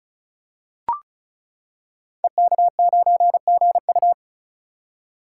Morse code, heard as a keyed tone, about 36 wpm, 700 Hz. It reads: EK9GU
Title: Code Groups - 5 Letter:Numbers - 35wpm